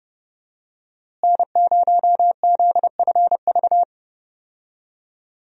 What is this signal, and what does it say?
Morse code, heard as a keyed tone, about 30 wpm, 700 Hz. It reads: N0ZFV
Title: USA Call Signs - 5 Characters - Repeated 2x - 30wpm